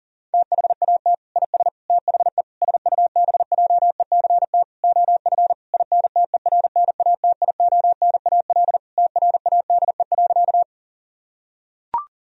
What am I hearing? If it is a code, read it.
Morse code, heard as a keyed tone, about 40 wpm, 700 Hz. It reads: THAT IS THE SUBJECT OF INTERNATIONAL TRADE.
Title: Sentences from Top 1000 Words - Review Part 4 - 40wpm